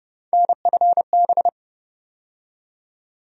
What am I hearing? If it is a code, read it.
Morse code, heard as a keyed tone, about 30 wpm, 700 Hz. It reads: NFB
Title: USA Call Sign Postfixes - 3 Letters 30wpm